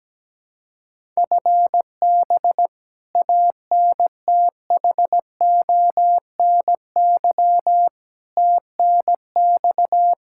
Morse code, heard as a keyed tone, about 17 wpm, 700 Hz. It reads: FB ANTHONY TNX
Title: Practice QSO - Exchange 3 - More Info - 17wpm